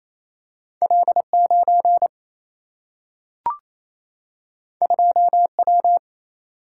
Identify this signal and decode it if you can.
Morse code, heard as a keyed tone, about 28 wpm, 700 Hz. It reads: L 9 2 W
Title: Single-Letter Single-Number 28wpm